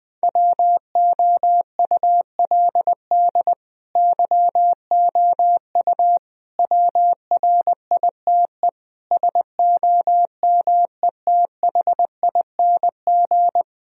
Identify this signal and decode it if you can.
Morse code, heard as a keyed tone, about 21 wpm, 700 Hz. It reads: WOULD YOU WRITE SOMETHING
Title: Sentences from Top 1000 Words - Review Part 2 - 20wpm